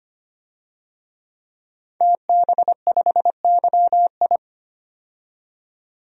Morse code, heard as a keyed tone, about 25 wpm, 700 Hz. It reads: TB5YI
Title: DXCC-100 Call Signs - 5 Characters - 25wpm